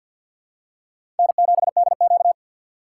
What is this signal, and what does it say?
Morse code, heard as a keyed tone, about 53 wpm, 700 Hz. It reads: N7DQ
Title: USA Call Signs - 4 Characters - 2x Character Spacing - 50wpm